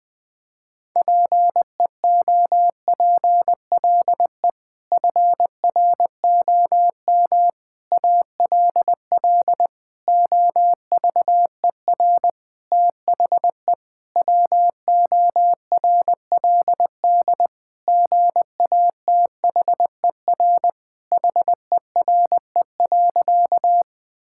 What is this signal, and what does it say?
Morse code, heard as a keyed tone, about 20 wpm, 700 Hz. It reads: PEOPLE FROM ALL OVER THE WORLD GATHER HERE.
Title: Sentences with 6-Word Phrases - 20wpm